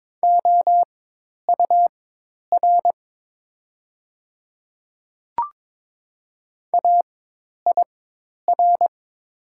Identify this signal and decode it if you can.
Morse code, heard as a keyed tone, about 22 wpm, spaced as for 10 wpm, 700 Hz. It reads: OUR AIR
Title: 4x Character Spacing - Top 200 Words - 22wpm